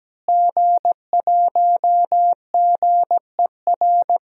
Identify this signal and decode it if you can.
Morse code, heard as a keyed tone, about 17 wpm, 700 Hz. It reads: G1GER
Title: DXCC-100 Call Signs - 5 Characters - Repeated 3x - 17wpm